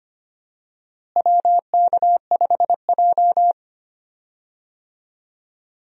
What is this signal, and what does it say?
Morse code, heard as a keyed tone, about 25 wpm, 700 Hz. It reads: WK5J
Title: USA Call Signs - 4 Characters - Repeated 3x - 25wpm